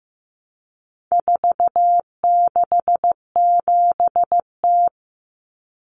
Morse code, heard as a keyed tone, about 15 wpm, 700 Hz. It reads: 467T
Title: Common Cut Numbers - 4-Digit 15wpm